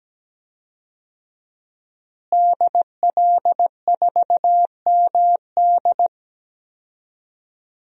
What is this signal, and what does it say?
Morse code, heard as a keyed tone, about 17 wpm, 700 Hz. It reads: DL4MD
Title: DXCC-100 Call Signs - 5 Characters - Repeated 3x - 17wpm